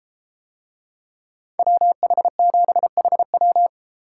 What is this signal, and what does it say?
Morse code, heard as a keyed tone, about 32 wpm, 700 Hz. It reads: WH7HW